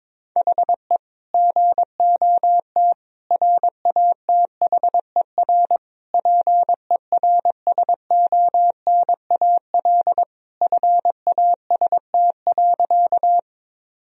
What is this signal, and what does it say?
Morse code, heard as a keyed tone, about 22 wpm, 700 Hz. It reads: HE GOT RATHER PERSONAL FAST.
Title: Sentences from Top 1000 Words - Review Part 4 - 22wpm